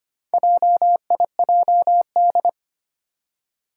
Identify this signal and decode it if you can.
Morse code, heard as a keyed tone, about 25 wpm, 700 Hz. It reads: JIJD
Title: Four Challenging Letter-Number Combinations 25wpm